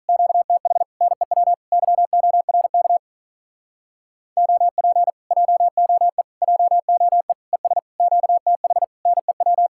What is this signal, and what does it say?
Morse code, heard as a keyed tone, about 48 wpm, 700 Hz. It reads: QTH NEW YORK OP JOE JOE ES QTH NEW